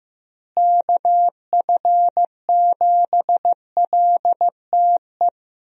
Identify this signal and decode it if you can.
Morse code, heard as a keyed tone, about 15 wpm, 700 Hz. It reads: KF7LTE
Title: USA Call Signs - 6 Characters - Repeated 2x - 15wpm